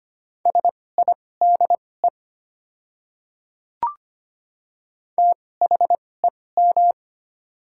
Morse code, heard as a keyed tone, about 25 wpm, spaced as for 18 wpm, 690 Hz. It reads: SIDE THEM
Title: 2x Character Spacing - Top 100 Words - 25wpm